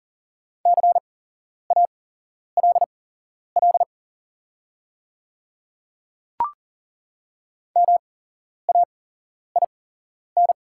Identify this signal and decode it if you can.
Morse code, heard as a keyed tone, about 40 wpm, spaced as for 11 wpm, 690 Hz. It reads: CALL MAIN
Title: 8x Character Spacing - Top 300 Words - 40wpm